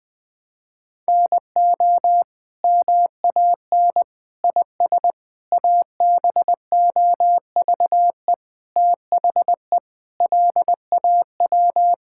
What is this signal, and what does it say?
Morse code, heard as a keyed tone, about 20 wpm, 700 Hz. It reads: NO MAN IS ABOVE THE LAW
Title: Sentences from Top 1000 Words - Review Part 5 - 20wpm